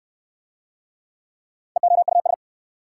Morse code, heard as a keyed tone, about 60 wpm, 700 Hz. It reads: EOUS